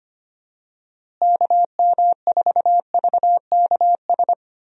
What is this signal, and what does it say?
Morse code, heard as a keyed tone, about 25 wpm, 700 Hz. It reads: KM4VKS
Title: USA Call Signs - 6 Characters - Repeated 2x - 25wpm